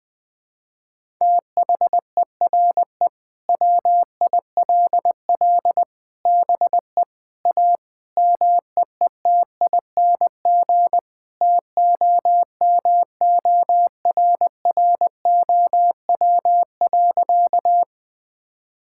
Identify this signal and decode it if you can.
Morse code, heard as a keyed tone, about 20 wpm, 700 Hz. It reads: THERE WILL BE A MEETING TOMORROW.
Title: Sentences with 4-Word Phrases - Spoken Beginning - 20wpm